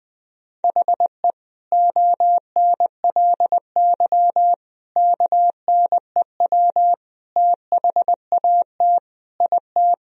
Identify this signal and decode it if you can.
Morse code, heard as a keyed tone, about 20 wpm, 700 Hz. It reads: HE ONLY KNEW THAT IT
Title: Sentences from Top 1000 Words - Review Part 2 - 20wpm